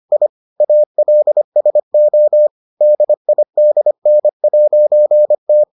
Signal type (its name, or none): Morse code